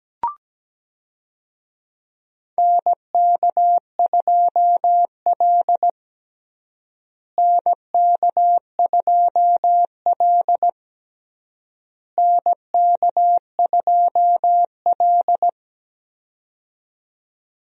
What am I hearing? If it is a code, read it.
Morse code, heard as a keyed tone, about 17 wpm, 700 Hz. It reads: NK2L NK2L NK2L
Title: USA Call Signs - 4 Characters - Repeated 3x - 17wpm